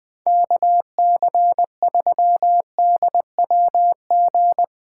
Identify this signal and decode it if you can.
Morse code, heard as a keyed tone, about 20 wpm, 700 Hz. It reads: KC3DWG
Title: USA Call Signs - 6 Characters - 2x Character Spacing - 20wpm